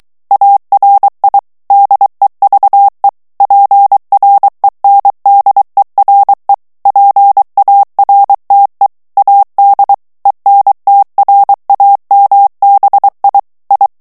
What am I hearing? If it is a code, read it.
Morse code, heard as a keyed tone, about 23 wpm, 800 Hz. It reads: ARI DEVE PRENDERE PARTE AD ENTRAMBI I
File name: ITALIEN texte clair 2 vitesse 22 mots minute